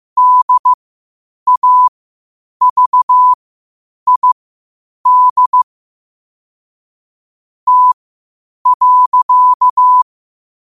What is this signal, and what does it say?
Morse code, heard as a keyed tone, about 15 wpm, spaced as for 8 wpm, 1000 Hz. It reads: DAVID T.
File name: qotdfiles / 8wpm